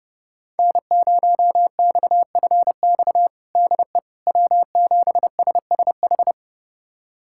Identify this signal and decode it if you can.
Morse code, heard as a keyed tone, about 30 wpm, 700 Hz. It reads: N0XFX DE W7SSH
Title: USA Callsign DE Callsign - 5 Characters - 30wpm